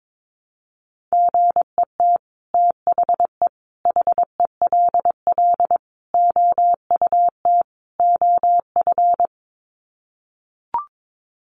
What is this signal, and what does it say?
Morse code, heard as a keyed tone, about 22 wpm, 700 Hz. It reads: GET THE HELL OUT OF
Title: Sets of 5 Words Encore2 22wpm